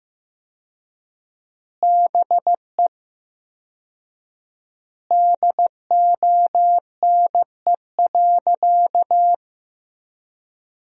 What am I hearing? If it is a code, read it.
Morse code, heard as a keyed tone, about 15 wpm, 700 Hz. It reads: BE DONE.
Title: 4x Word-Spacing - Sentences from Top 500 Words - 15wpm